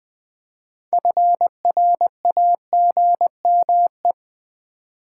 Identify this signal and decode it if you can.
Morse code, heard as a keyed tone, about 20 wpm, 700 Hz. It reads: FRAGME
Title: Build A Word - 3 Letters At A Time - 8-Letter Words - 20wpm